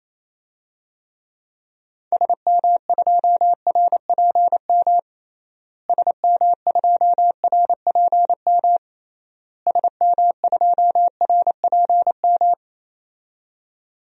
Morse code, heard as a keyed tone, about 28 wpm, 700 Hz. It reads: SM2RPM SM2RPM SM2RPM
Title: DXCC-100 Call Signs - 6 Characters - Repeated 3x - 28wpm